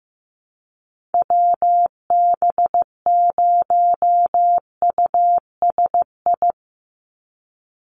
Morse code, heard as a keyed tone, about 15 wpm, 700 Hz. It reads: WB0USI
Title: NA SOTA Activator Callsigns 15wpm